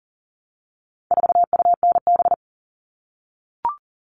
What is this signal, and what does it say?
Morse code, heard as a keyed tone, about 40 wpm, 700 Hz. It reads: 4UNB
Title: All Cut Numbers - 4-Digit 40wpm